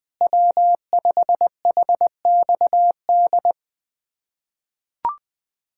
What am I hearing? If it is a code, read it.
Morse code, heard as a keyed tone, about 20 wpm, 700 Hz. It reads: W5HXD
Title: USA Callsign DE Callsign 20wpm